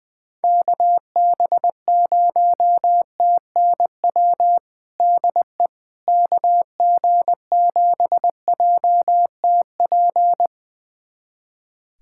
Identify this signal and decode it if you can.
Morse code, heard as a keyed tone, about 20 wpm, 700 Hz. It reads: KB0TNW DE KG7JTP
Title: USA Callsign DE Callsign - 6 Characters - 20wpm